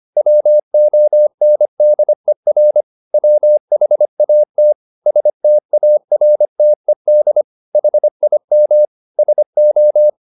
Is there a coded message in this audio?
Yes: Morse code from 0.2 to 10.1 s, heard as a keyed tone, about 25 wpm, 600 Hz. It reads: WONDER WHAT STARTED HIM SO